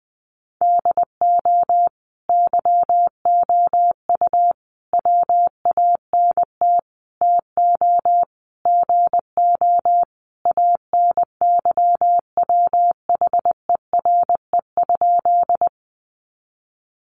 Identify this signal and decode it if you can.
Morse code, heard as a keyed tone, about 20 wpm, 700 Hz. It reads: DO YOU WANT TO GO ANYWHERE?
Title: Sentences from Top 1400 Words 20wpm